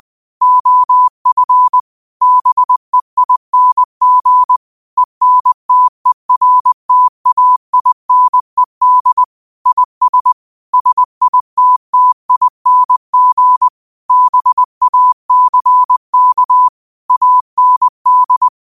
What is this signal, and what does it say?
Morse code, heard as a keyed tone, about 20 wpm, 1000 Hz. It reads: OF BEING ENTERTAINED IS SITTING BACK AND
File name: qotdfiles / 20wpm